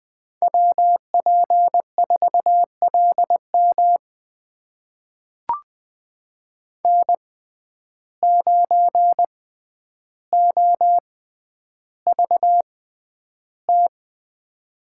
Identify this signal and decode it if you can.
Morse code, heard as a keyed tone, about 20 wpm, spaced as for 10 wpm, 700 Hz. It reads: WP4LM N9OVT
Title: USA Call Signs - 5 Characters - 6x Character Spacing - 20wpm